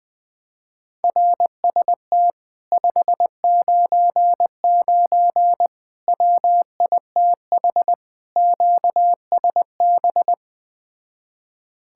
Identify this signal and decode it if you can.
Morse code, heard as a keyed tone, about 20 wpm, 700 Hz. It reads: RST 599 WITH QSB
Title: Speed-Racing Common QSO Phrases - 17wpm